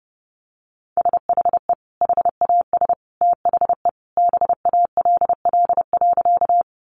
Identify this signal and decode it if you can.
Morse code, heard as a keyed tone, about 30 wpm, 700 Hz. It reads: SHE HAS THE BALL.
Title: Sentences from Top 900 Words 30wpm